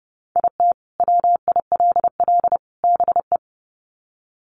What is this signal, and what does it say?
Morse code, heard as a keyed tone, about 30 wpm, 700 Hz. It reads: IT WILL BE
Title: Sets of 3 Words 30wpm